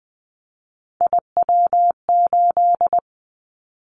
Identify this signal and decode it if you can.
Morse code, heard as a keyed tone, about 20 wpm, 700 Hz. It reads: IW8
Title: Callsign Prefixes for Top 50 DXCC - 20wpm